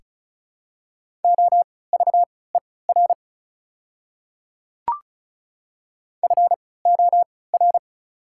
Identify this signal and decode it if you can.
Morse code, heard as a keyed tone, about 36 wpm, spaced as for 20 wpm, 690 Hz. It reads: OVER FOR
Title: 3x Character Spacing - Top 100 Words - 35wpm